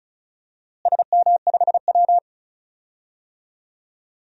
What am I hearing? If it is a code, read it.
Morse code, heard as a keyed tone, about 34 wpm, 700 Hz. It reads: SM5W
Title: DXCC-100 Call Signs - 4 Characters - Repeated 2x - 35wpm